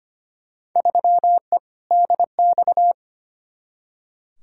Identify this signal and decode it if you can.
Morse code, heard as a keyed tone, about 25 wpm, 700 Hz. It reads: 3E DX
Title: Field Day Exchanges Encore - 25wpm